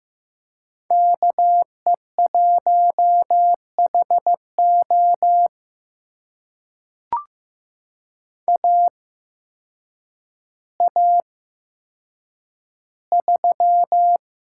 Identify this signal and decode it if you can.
Morse code, heard as a keyed tone, about 15 wpm, spaced as for 8 wpm, 700 Hz. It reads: KE1HO AA3
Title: USA Call Signs - 5 Characters - 8x Character Spacing - 15wpm